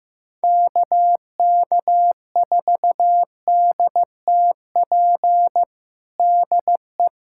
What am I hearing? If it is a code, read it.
Morse code, heard as a keyed tone, about 15 wpm, 700 Hz. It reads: KK4DTP DE
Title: USA Callsign DE Callsign - 6 Characters - 15wpm